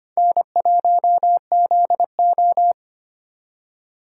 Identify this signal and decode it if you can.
Morse code, heard as a keyed tone, about 34 wpm, spaced as for 30 wpm, 700 Hz. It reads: N1ZO
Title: Speed Racing - USA 1x2 Call Signs 25wpm